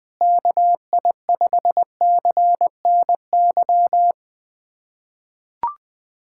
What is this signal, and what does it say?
Morse code, heard as a keyed tone, about 20 wpm, 700 Hz. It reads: KI5CNY
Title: USA Callsign DE Callsign - 6 Characters - 20wpm